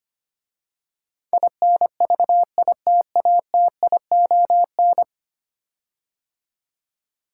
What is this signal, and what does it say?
Morse code, heard as a keyed tone, about 25 wpm, 700 Hz. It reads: INVITATION
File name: 10 Letter Words - First 3 Letters Given - 25wpm